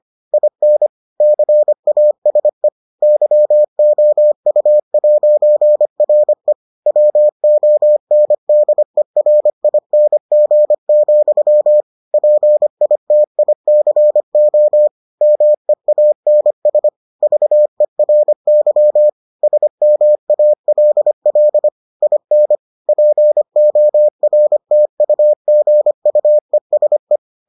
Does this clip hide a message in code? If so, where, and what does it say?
Yes: Morse code from 0.3 to 27.2 s, heard as a keyed tone, about 25 wpm, 600 Hz. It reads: IN CASE YOU'RE WONDERING, PITICO MEANS VERY SMALL IN PORTUGUESE